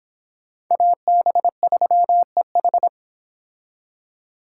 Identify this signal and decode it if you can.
Morse code, heard as a keyed tone, about 26 wpm, 700 Hz. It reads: AB3EH